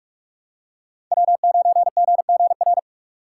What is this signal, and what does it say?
Morse code, heard as a keyed tone, about 46 wpm, 700 Hz. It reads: W9GGR
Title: USA Call Signs - 5 Characters - 2x Character Spacing - 45wpm